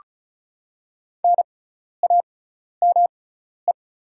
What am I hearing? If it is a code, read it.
Morse code, heard as a keyed tone, about 36 wpm, spaced as for 12 wpm, 700 Hz. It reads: NAME
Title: 6x Character Spacing - Top 300 Words - 35wpm